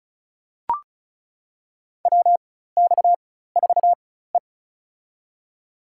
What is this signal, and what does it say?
Morse code, heard as a keyed tone, about 34 wpm, spaced as for 16 wpm, 700 Hz. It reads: WX4E
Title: USA Call Signs - 4x Character Spacing - 35wpm